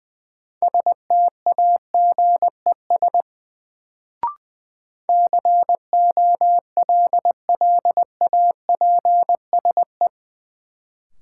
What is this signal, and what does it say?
Morse code, heard as a keyed tone, about 20 wpm, 700 Hz. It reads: STAGES COLLAPSE
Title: Sets of 5 Words Encore - New Word Drill - 20wpm